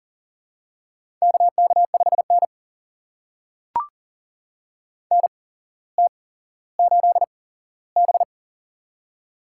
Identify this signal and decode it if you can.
Morse code, heard as a keyed tone, about 42 wpm, spaced as for 17 wpm, 700 Hz. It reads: KK5N NT8B
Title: USA Call Signs - 4 Characters - 8x Character Spacing - 40wpm